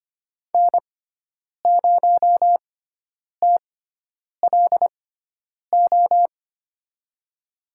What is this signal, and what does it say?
Morse code, heard as a keyed tone, about 25 wpm, spaced as for 9 wpm, 700 Hz. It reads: N0TLO
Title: USA Call Signs - 5 Characters - 6x Character Spacing - 25wpm